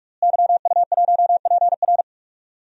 Morse code, heard as a keyed tone, about 43 wpm, 700 Hz. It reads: YU1PR
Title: DXCC-100 Call Signs - 5 Characters - Repeated 2x - 45wpm